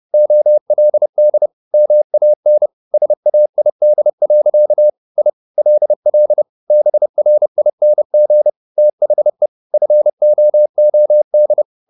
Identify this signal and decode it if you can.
Morse code, heard as a keyed tone, about 30 wpm, 600 Hz. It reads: OLD MAN SAID. I LL BRING THE FOOD